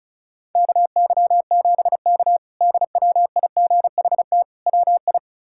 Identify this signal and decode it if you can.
Morse code, heard as a keyed tone, about 36 wpm, 700 Hz. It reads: KY7K DWIGHT WI
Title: K1USN SST QSO Practice - 35wpm